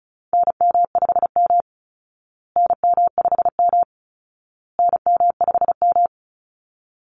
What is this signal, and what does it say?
Morse code, heard as a keyed tone, about 34 wpm, 700 Hz. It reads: NM5M NM5M NM5M
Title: Call Signs Repeated 2x - 35wpm